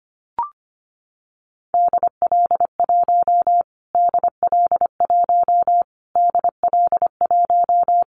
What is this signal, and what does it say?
Morse code, heard as a keyed tone, about 25 wpm, 700 Hz. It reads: DL1 DL1 DL1
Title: Callsign Prefixes for Top 25 DXCC - Repeated 3x - 25wpm